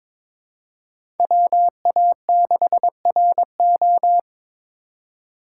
Morse code, heard as a keyed tone, about 22 wpm, 700 Hz. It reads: WA6RO
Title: USA Call Signs - 5 Characters - 3x Character Spacing - 22wpm